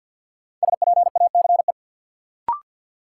Morse code, heard as a keyed tone, about 53 wpm, 700 Hz. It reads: SPACE
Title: Sentences from Top 1500 Words - New Word Drill Alt - 50wpm